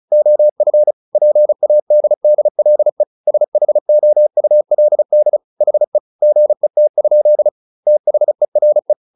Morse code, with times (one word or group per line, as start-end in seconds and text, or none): 0.1-0.9 OF
1.2-3.0 PADDLE
3.3-5.4 SHOULD
5.6-6.0 HE
6.2-7.5 GET?
7.9-8.9 THERE